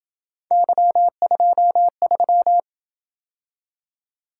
Morse code, heard as a keyed tone, about 27 wpm, 700 Hz. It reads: Y23